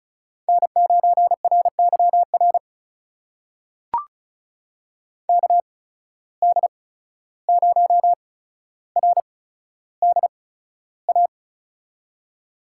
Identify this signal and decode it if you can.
Morse code, heard as a keyed tone, about 36 wpm, spaced as for 14 wpm, 700 Hz. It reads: N9RYR KD0RDA
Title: USA Call Signs - 8x Character Spacing - 35wpm